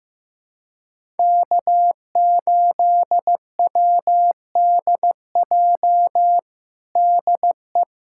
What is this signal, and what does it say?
Morse code, heard as a keyed tone, about 16 wpm, 700 Hz. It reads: K8WDJ DE
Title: USA Callsign DE Callsign - 5 Characters - 15wpm